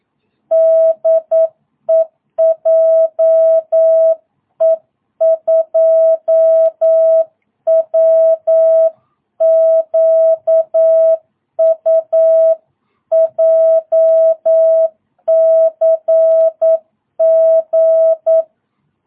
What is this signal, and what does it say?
Morse code, heard as a keyed tone, about 9 wpm, 660 Hz. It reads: DE JE2WQU JCG
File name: jh8cbh / 2022cw